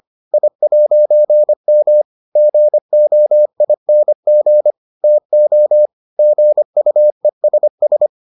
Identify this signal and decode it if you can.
Morse code, heard as a keyed tone, about 25 wpm, 600 Hz. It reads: I'M GOING TO GUESS